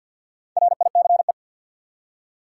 Morse code, heard as a keyed tone, about 48 wpm, 700 Hz. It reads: RICE
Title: Four Letter Words - 50wpm